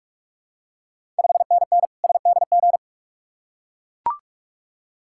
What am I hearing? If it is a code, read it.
Morse code, heard as a keyed tone, about 44 wpm, 690 Hz. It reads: 5NN SDG
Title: Ontario QSO Party Exchange 45wpm